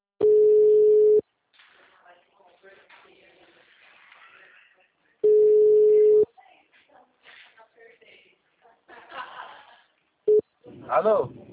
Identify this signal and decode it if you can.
Ringback tone (Europe, 425 Hz), with speech.